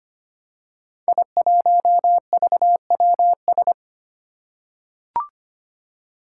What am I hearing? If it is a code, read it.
Morse code, heard as a keyed tone, about 25 wpm, 700 Hz. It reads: I1VWS
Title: Five Challenging Letter-Number Combinations 25wpm